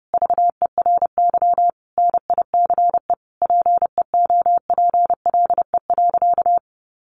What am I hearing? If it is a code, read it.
Morse code, heard as a keyed tone, about 30 wpm, 700 Hz. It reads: VERY NICE PEOPLE.
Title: Sentences from Top 1100 Words 30wpm